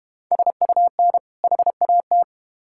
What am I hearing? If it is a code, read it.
Morse code, heard as a keyed tone, about 32 wpm, 700 Hz. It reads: SUN HAT